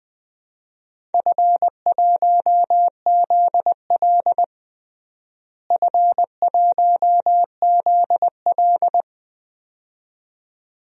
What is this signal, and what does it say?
Morse code, heard as a keyed tone, about 20 wpm, 700 Hz. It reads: F1ZL F1ZL
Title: DXCC-100 Call Signs - 4 Characters - Repeated 2x - 20wpm